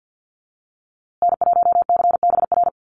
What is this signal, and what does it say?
Morse code, heard as a keyed tone, about 50 wpm, 700 Hz. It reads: N1CBR
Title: USA Call Signs Repeated 3x 50wpm